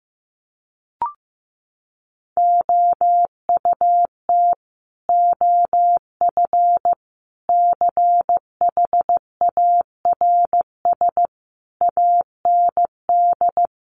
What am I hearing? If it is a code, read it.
Morse code, heard as a keyed tone, about 15 wpm, 700 Hz. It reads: OUT OF CHARS AND
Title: A Sound Of Thunder 15wpm